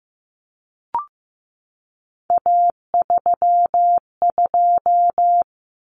Morse code, heard as a keyed tone, about 15 wpm, 700 Hz. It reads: A32
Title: Common Cut Numbers - 3-Digit 15wpm